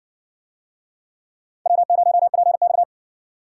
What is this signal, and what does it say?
Morse code, heard as a keyed tone, about 57 wpm, 700 Hz. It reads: W0PX